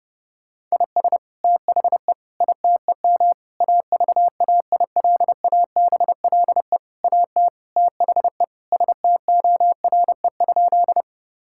Morse code, heard as a keyed tone, about 30 wpm, 700 Hz. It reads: IS THE ITEM AVAILABLE AT THE STORE?
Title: Sentences from Top 1000 Words - Review Part 6 - 30wpm